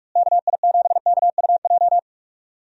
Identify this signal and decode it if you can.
Morse code, heard as a keyed tone, about 46 wpm, 700 Hz. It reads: KI7KUJ
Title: USA Call Signs - 6 Characters - Repeated 3x - 45wpm